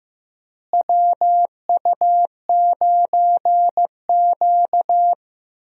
Morse code, heard as a keyed tone, about 15 wpm, 700 Hz. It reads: WU9Q
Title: USA Call Signs - 4 Characters - Repeated 2x - 15wpm